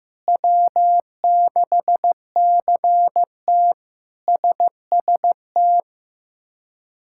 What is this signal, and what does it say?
Morse code, heard as a keyed tone, about 15 wpm, 700 Hz. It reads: W6CT SST
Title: K1USN SST QSO Practice - 15wpm